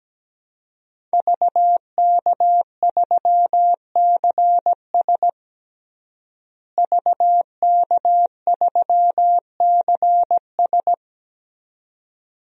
Morse code, heard as a keyed tone, about 17 wpm, 700 Hz. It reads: VK3CS VK3CS
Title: DXCC-100 Call Signs - 5 Characters - Repeated 2x - 17wpm